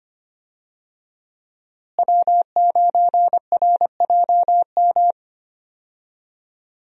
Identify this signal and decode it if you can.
Morse code, heard as a keyed tone, about 25 wpm, 700 Hz. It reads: W9RJM
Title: USA Call Signs - 5 Characters - Repeated 2x - 25wpm